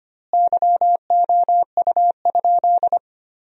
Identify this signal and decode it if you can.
Morse code, heard as a keyed tone, about 25 wpm, 700 Hz. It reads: YOU?
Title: 2x Word-Spacing - Sentences from Top 400 Words - 25wpm